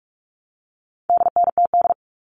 Morse code, heard as a keyed tone, about 44 wpm, 700 Hz. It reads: DNTD
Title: All Cut Numbers - 4-Digit 45wpm